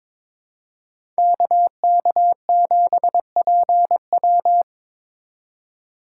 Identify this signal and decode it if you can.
Morse code, heard as a keyed tone, about 21 wpm, 700 Hz. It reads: KK7PW
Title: USA Call Signs - 5 Characters - 2x Character Spacing - 22wpm